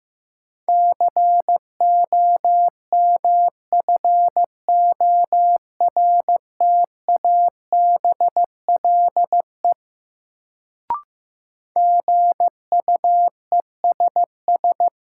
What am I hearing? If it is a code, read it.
Morse code, heard as a keyed tone, about 15 wpm, 700 Hz. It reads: COMFORTABLE GUESS
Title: Sentences from Top 1500 Words - New Word Drill Alt - 15wpm